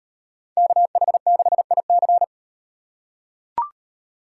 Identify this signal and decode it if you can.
Morse code, heard as a keyed tone, about 38 wpm, 700 Hz. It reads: KH6IC